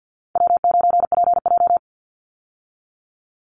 Morse code, heard as a keyed tone, about 50 wpm, 700 Hz. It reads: W9PJ
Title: Easy Call Signs - 50wpm